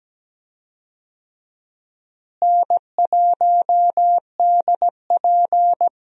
Morse code, heard as a keyed tone, about 17 wpm, 700 Hz. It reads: N1DP
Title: USA Call Signs - 4 Characters - Repeated 2x - 17wpm